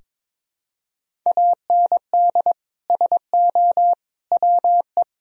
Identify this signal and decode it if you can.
Morse code, heard as a keyed tone, about 22 wpm, 700 Hz. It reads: AND SO WE
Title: Sentences from Top 1000 Words - Review Part 1 - 22wpm